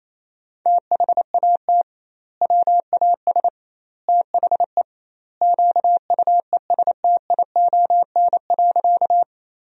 Morse code, heard as a keyed tone, about 28 wpm, 700 Hz. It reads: THAT WAS THE QUESTION.
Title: 2x Word-Spacing - Sentences from Top 400 Words - 28wpm